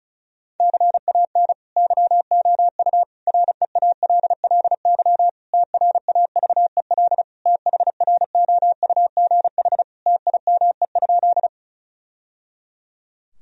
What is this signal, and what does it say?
Morse code, heard as a keyed tone, about 36 wpm, 700 Hz. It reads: CAN YOU REALLY TRAVEL THROUGH TIME?
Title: Sentences from Top 1000 Words - Review Part 2 - 35wpm